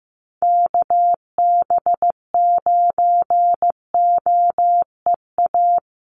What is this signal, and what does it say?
Morse code, heard as a keyed tone, about 15 wpm, 700 Hz. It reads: KB9OEA
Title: USA Call Signs Repeated 2x 15wpm